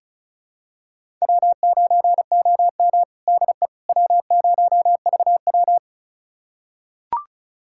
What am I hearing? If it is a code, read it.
Morse code, heard as a keyed tone, about 36 wpm, 700 Hz. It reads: W9OM DE W0VW
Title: USA Callsign DE Callsign - 4 Characters - 35wpm